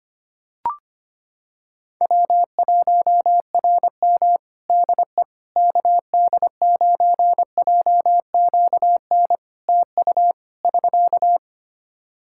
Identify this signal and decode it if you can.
Morse code, heard as a keyed tone, about 25 wpm, 700 Hz. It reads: W1RM DE KD9JQN TU <SK>
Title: K1USN SST QSO Practice 25wpm